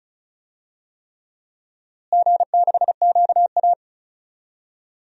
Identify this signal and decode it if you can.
Morse code, heard as a keyed tone, about 36 wpm, 700 Hz. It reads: G6QA
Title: DXCC-100 Call Signs - 4 Characters - Repeated 3x - 35wpm